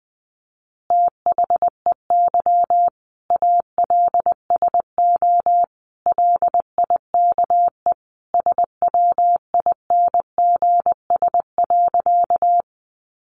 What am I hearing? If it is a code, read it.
Morse code, heard as a keyed tone, about 20 wpm, 700 Hz. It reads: THEY ALSO LIKE SWINGS.
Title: Easy Phrases 20wpm